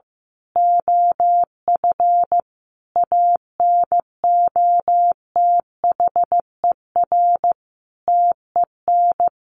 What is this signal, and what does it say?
Morse code, heard as a keyed tone, about 15 wpm, 700 Hz. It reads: OF ANOTHER TEN
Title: A Sound Of Thunder 15wpm